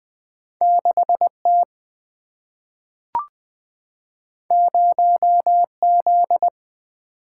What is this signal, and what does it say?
Morse code, heard as a keyed tone, about 20 wpm, 700 Hz. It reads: 6 T 0 Z
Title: Single-Letter Single-Number 20wpm